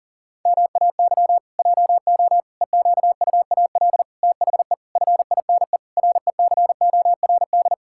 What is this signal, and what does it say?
Morse code, heard as a keyed tone, about 40 wpm, 700 Hz. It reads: MAY JO EQUAL THE FINE RECORD
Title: Pangram Challenge v2 - 40wpm